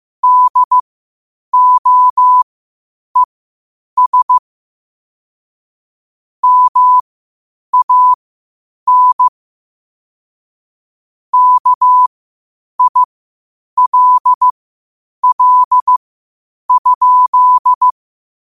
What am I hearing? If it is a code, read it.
Morse code, heard as a keyed tone, about 15 wpm, spaced as for 8 wpm, 1000 Hz. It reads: DOES MAN KILL?
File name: qotdfiles / 8wpm